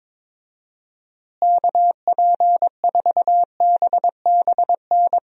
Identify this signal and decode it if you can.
Morse code, heard as a keyed tone, about 22 wpm, 700 Hz. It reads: KP4BBN
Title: DXCC-100 Call Signs - 6 Characters - Repeated 2x - 22wpm